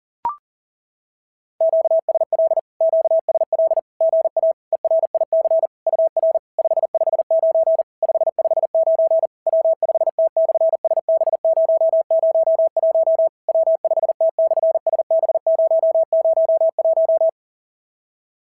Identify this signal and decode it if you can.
Morse code, heard as a keyed tone, about 41 wpm, 650 Hz. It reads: QSL QSL GA ERIC UR 559 559 W5T/SB001 W5T/SB001
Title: SOTA QSO Practice 40wpm